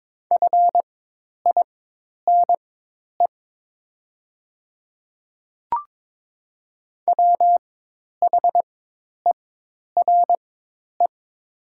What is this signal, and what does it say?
Morse code, heard as a keyed tone, about 22 wpm, spaced as for 10 wpm, 700 Hz. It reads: FINE WHERE
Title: 4x Character Spacing - Top 500 Words - 22wpm